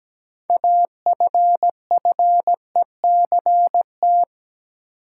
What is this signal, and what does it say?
Morse code, heard as a keyed tone, about 17 wpm, 700 Hz. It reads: AFFECT
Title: Sentences from Top 900 Words - New Word Drill - 17wpm